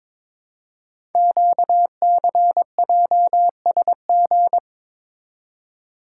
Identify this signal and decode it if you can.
Morse code, heard as a keyed tone, about 22 wpm, 700 Hz. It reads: QCJSG
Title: Code Groups - 5 Letters - 22wpm